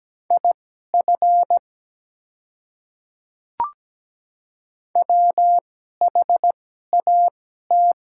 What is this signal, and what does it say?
Morse code, heard as a keyed tone, about 17 wpm, spaced as for 12 wpm, 700 Hz. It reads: IF WHAT
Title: 2x Character Spacing - Top 100 Words - 17wpm